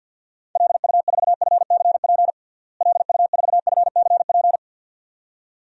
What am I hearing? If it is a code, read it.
Morse code, heard as a keyed tone, about 51 wpm, 700 Hz. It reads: LU4FCP LU4FCP
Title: DXCC-100 Call Signs - 6 Characters - Repeated 2x - 50wpm